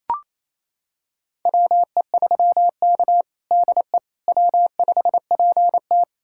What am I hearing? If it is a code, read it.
Morse code, heard as a keyed tone, about 28 wpm, 700 Hz. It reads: WE3K DE W5PT
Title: USA Callsign DE Callsign - 4 Characters - 28wpm